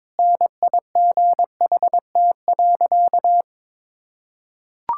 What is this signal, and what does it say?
Morse code, heard as a keyed tone, about 22 wpm, 700 Hz. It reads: NIGHT.
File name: practice / 22wpm / Sentences with 8-Word Phrases - 22wpm